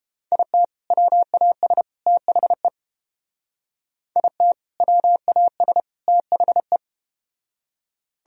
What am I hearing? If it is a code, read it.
Morse code, heard as a keyed tone, about 30 wpm, 700 Hz. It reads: IT WAS THE IT WAS THE